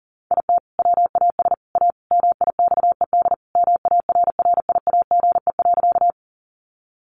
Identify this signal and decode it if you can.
Morse code, heard as a keyed tone, about 40 wpm, 700 Hz. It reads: IT WAS A MIXED MARRIAGE.
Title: Sentences from Top 1500 Words 40wpm